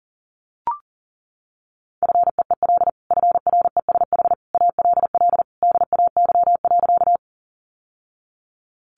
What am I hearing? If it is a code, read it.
Morse code, heard as a keyed tone, about 40 wpm, 700 Hz. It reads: FEEL FRESH ALL DAY.
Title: Sentences from Top 1200 Words 40wpm